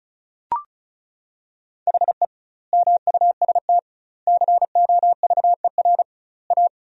Morse code, heard as a keyed tone, about 36 wpm, 700 Hz. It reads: HE MUST COVER A
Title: 2x Word-Spacing - Sentences from Top 500 Words - 35wpm